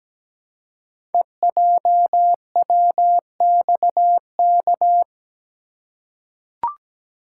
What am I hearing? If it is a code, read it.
Morse code, heard as a keyed tone, about 17 wpm, 700 Hz. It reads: EJWXK
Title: Code Groups - 5 Letters - 17wpm